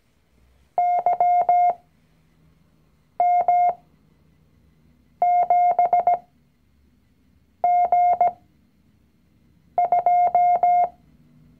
Morse code, heard as a keyed tone, about 17 wpm, spaced as for 5 wpm, 710 Hz. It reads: YM7G2
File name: CW-Online-Kurs / Anfänger / Lektion 019